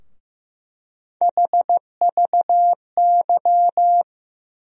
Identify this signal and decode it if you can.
Morse code, heard as a keyed tone, about 15 wpm, 700 Hz. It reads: HVY
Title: USA Call Sign Postfixes - 3 Letters 15wpm